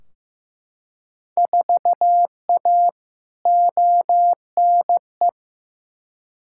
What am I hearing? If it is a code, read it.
Morse code, heard as a keyed tone, about 15 wpm, 700 Hz. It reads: 4A ONE
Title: Field Day Exchanges Encore - 15wpm